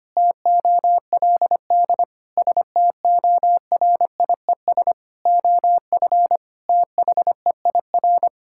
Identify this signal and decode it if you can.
Morse code, heard as a keyed tone, about 25 wpm, 700 Hz. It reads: TOLD STORIES OF THEIR
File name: practice / 25wpm / Sentences with 2-Word Phrases - 25wpm